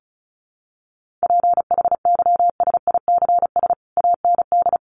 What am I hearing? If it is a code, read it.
Morse code, heard as a keyed tone, about 34 wpm, 700 Hz. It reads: PHYSICS AND
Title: Sentences from Top 1500 Words 35wpm